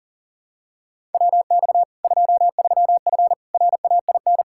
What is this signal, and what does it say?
Morse code, heard as a keyed tone, about 40 wpm, 700 Hz. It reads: WX 23F RAIN
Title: Practice QSO - Exchange 2 - Info - 40wpm